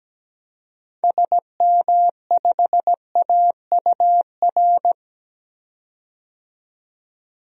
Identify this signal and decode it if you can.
Morse code, heard as a keyed tone, about 17 wpm, 700 Hz. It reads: SM5AUR
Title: DXCC-100 Call Signs - 6 Characters - 17wpm